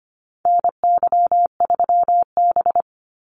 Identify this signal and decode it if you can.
Morse code, heard as a keyed tone, about 25 wpm, 700 Hz. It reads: NY3B
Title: Call Signs Repeated 3x - 25wpm